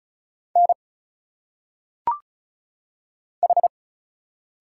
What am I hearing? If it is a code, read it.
Morse code, heard as a keyed tone, about 34 wpm, 700 Hz. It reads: N H
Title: Single Letters - Phonetic Pronunciation 35wpm